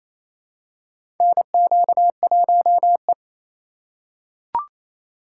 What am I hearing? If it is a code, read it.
Morse code, heard as a keyed tone, about 28 wpm, 700 Hz. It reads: NQ1E
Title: USA Call Signs - 4 Characters - 4x Character Spacing - 28wpm